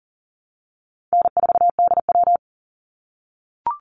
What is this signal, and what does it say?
Morse code, heard as a keyed tone, about 40 wpm, 700 Hz. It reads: N4DW
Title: Call Signs - 40wpm